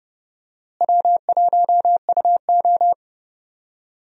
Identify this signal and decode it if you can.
Morse code, heard as a keyed tone, about 31 wpm, 700 Hz. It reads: W1UO
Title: USA Call Signs - 4 Characters - Repeated 2x - 30wpm